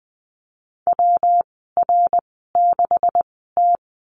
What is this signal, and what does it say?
Morse code, heard as a keyed tone, about 19 wpm, spaced as for 14 wpm, 700 Hz. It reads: WR6T
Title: 173 Lesson 28g - Number 9 - Call Signs 20wpm